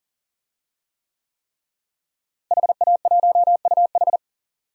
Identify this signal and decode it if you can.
Morse code, heard as a keyed tone, about 40 wpm, 700 Hz. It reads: HA1UH
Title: DXCC-100 Call Signs - 5 Characters - Repeated 3x - 40wpm